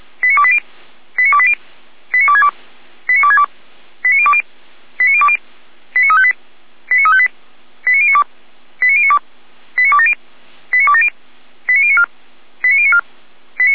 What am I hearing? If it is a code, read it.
Five-tone selective call (ZVEI), heard as quick runs of five tones, tones of about 70 ms. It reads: 89280 89280 89262 89262 89020 89020 89358 89358 89092 89092 89270 89270 89094 89094